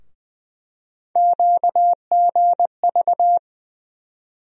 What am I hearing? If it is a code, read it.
Morse code, heard as a keyed tone, about 20 wpm, 700 Hz. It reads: QGV
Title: USA Call Sign Postfixes - 3 Letters 20wpm